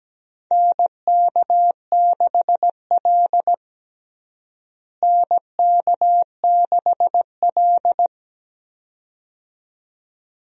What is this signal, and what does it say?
Morse code, heard as a keyed tone, about 17 wpm, 700 Hz. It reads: NK6L NK6L
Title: USA Call Signs - 4 Characters - Repeated 2x - 17wpm